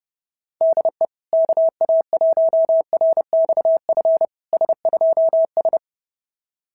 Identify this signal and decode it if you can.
Morse code, heard as a keyed tone, about 28 wpm, 660 Hz. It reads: DE KA1RXF S2S
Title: SOTA QSO Practice 30wpm